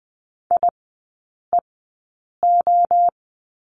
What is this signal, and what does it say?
Morse code, heard as a keyed tone, about 20 wpm, spaced as for 8 wpm, 700 Hz. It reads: IEO
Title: 100 Lesson 17e - Letter u - Three Characters 20wpm